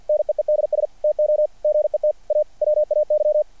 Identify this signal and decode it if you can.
Morse code, heard as a keyed tone, about 50 wpm, 600 Hz. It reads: NEEDS TO GET A WAY
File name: archived / CRSnet